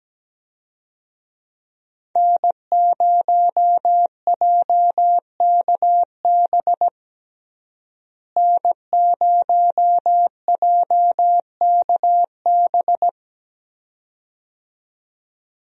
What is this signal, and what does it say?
Morse code, heard as a keyed tone, about 17 wpm, 700 Hz. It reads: N0JKB N0JKB
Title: USA Call Signs - 5 Characters - Repeated 2x - 17wpm